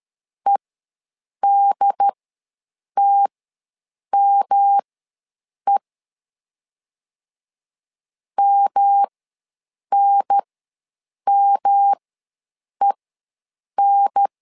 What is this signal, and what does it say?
Morse code, heard as a keyed tone, about 13 wpm, spaced as for 7 wpm, 800 Hz. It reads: EDTME MNMEN
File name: LEZIONI / 3 MWRO / 3 M ws 12wpm 1745